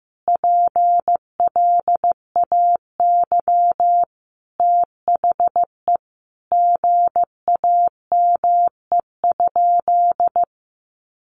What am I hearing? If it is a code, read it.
Morse code, heard as a keyed tone, about 15 wpm, 700 Hz. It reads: PLAY THE GAME?
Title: Sentences from Top 1400 Words 15wpm